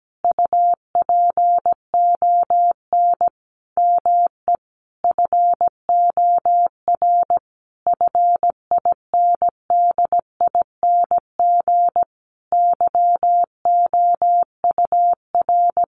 Morse code, heard as a keyed tone, about 17 wpm, 700 Hz. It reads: UPON ME FOR FINDING YOUR
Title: Sentences from Top 1300 Words 17wpm